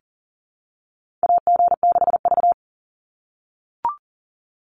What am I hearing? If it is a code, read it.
Morse code, heard as a keyed tone, about 40 wpm, 700 Hz. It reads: AG6V
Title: NA SOTA Chaser Callsigns 40wpm